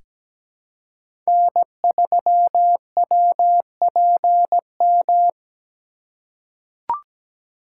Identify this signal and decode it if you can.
Morse code, heard as a keyed tone, about 17 wpm, 700 Hz. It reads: N3WPM
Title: USA Call Signs - 3x Character Spacing - 17wpm